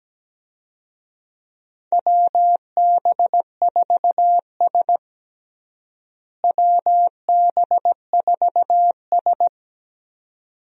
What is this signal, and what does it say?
Morse code, heard as a keyed tone, about 17 wpm, 700 Hz. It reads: WB4S WB4S
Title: USA Call Signs - 4 Characters - Repeated 2x - 17wpm